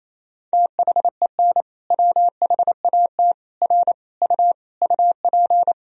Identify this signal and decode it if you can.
Morse code, heard as a keyed tone, about 28 wpm, 700 Hz. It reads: THEN WHAT R U UP
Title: Sentences from Top 100 Words and Abbreviations 28wpm